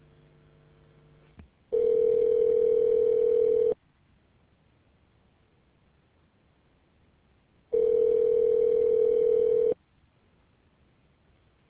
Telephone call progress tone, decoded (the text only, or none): ringback tone